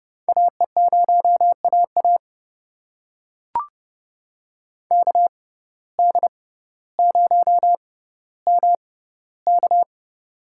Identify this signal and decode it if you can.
Morse code, heard as a keyed tone, about 30 wpm, spaced as for 15 wpm, 700 Hz. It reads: AE0AA KD0MK
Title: USA Call Signs - 5 Characters - 6x Character Spacing - 30wpm